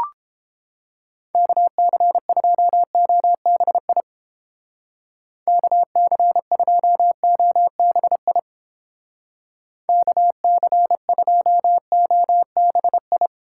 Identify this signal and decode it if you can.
Morse code, heard as a keyed tone, about 29 wpm, 700 Hz. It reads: KC2OBI KC2OBI KC2OBI